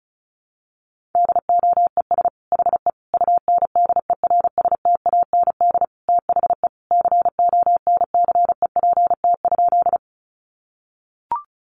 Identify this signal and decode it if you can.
Morse code, heard as a keyed tone, about 34 wpm, 700 Hz. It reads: DOES HE UNDERSTAND THE CONCEPT?
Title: Sentences from Top 1100 Words 35wpm